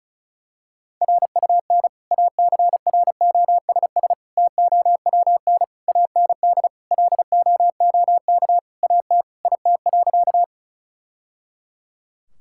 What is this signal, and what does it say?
Morse code, heard as a keyed tone, about 36 wpm, 690 Hz. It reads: RUN ACROSS TOWN AND LOOK AT IT.
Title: Sentences from Top 1000 Words - Review Part 5 - 35wpm